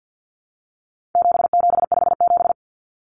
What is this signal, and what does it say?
Morse code, heard as a keyed tone, about 50 wpm, 700 Hz. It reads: 7757
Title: Numbers - 4-Digit 50wpm